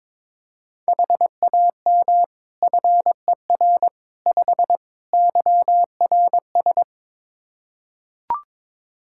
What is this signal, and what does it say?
Morse code, heard as a keyed tone, about 22 wpm, 700 Hz. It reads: HAM FER 5 YRS
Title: K1USN SST QSO Practice 22wpm